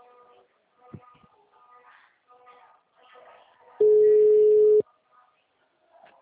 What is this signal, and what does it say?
Ringback tone (Europe, 425 Hz).
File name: out- -2008- - - .92116